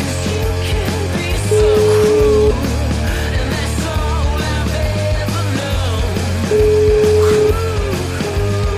Ringback tone (Europe, 425 Hz), with noise.